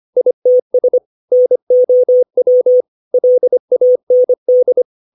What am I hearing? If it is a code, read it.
Morse code, heard as a keyed tone, about 25 wpm, 500 Hz. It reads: ITS NOW LAND
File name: 632 words / 632 Words 25WPM